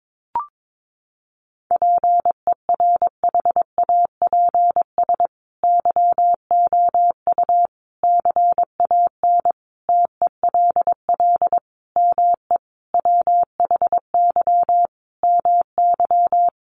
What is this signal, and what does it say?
Morse code, heard as a keyed tone, about 22 wpm, 700 Hz. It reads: PERHAPS YOU CAN TELL ME WHY MY
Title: Sentences from Top 1200 Words 22wpm